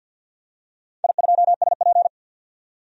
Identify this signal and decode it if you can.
Morse code, heard as a keyed tone, about 51 wpm, 700 Hz. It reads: I2SP
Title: DXCC-100 Call Signs - 4 Characters - 50wpm